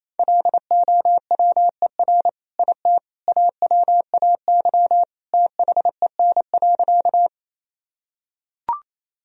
Morse code, heard as a keyed tone, about 27 wpm, 700 Hz. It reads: LOWER IT AWAY THEN.
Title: Sentences from Top 1000 Words - Review Part 4 - 28wpm